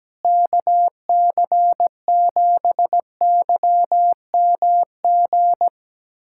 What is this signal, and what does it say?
Morse code, heard as a keyed tone, about 17 wpm, 700 Hz. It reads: KC7YMG
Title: USA Callsign DE Callsign - 6 Characters - 17wpm